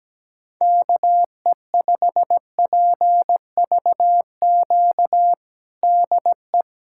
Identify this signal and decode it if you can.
Morse code, heard as a keyed tone, about 17 wpm, 700 Hz. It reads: KE5PVQ DE
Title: USA Callsign DE Callsign 17wpm